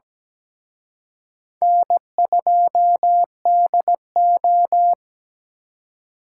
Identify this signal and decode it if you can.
Morse code, heard as a keyed tone, about 20 wpm, 700 Hz. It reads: N2DO
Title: Speed Racing - USA 1x2 Call Signs 17wpm